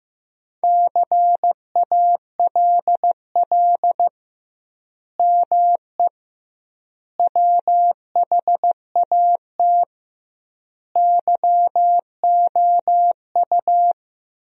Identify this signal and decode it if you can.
Morse code, heard as a keyed tone, about 15 wpm, 700 Hz. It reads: CALL ME WHAT YOU
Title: 2x Word-Spacing - Sentences from Top 200 Words - 15wpm